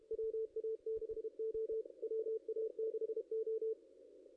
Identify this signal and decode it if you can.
Morse code, heard as a keyed tone, about 31 wpm, 440 Hz. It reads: WA6O WA6O
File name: contests / ss-cw-k5tr-op-n5rz -0400